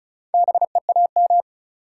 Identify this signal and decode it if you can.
Morse code, heard as a keyed tone, about 48 wpm, spaced as for 42 wpm, 700 Hz. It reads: BEAM
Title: Speed-Racing Top 90 QSO Elements 35wpm